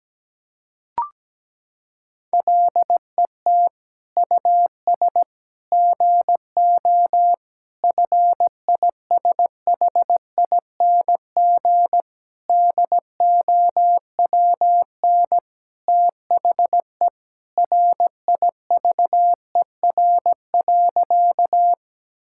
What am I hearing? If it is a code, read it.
Morse code, heard as a keyed tone, about 17 wpm, 700 Hz. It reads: LET US GO FISHING DOWN THE RIVER.
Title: Sentences from Top 1000 Words - Review Part 5 - 17wpm